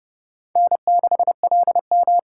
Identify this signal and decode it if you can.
Morse code, heard as a keyed tone, about 30 wpm, 700 Hz. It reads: N6LM
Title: USA Call Signs - 4 Characters - 6x Character Spacing - 30wpm